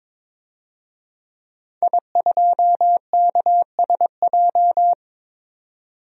Morse code, heard as a keyed tone, about 22 wpm, 700 Hz. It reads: I2KSJ
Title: DXCC-100 Call Signs - 5 Characters - 22wpm